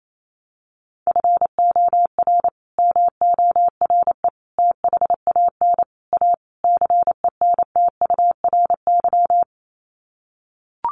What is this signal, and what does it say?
Morse code, heard as a keyed tone, about 28 wpm, 700 Hz. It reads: FOR MORE THAN A CENTURY
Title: Sets of 5 Words Encore2 28wpm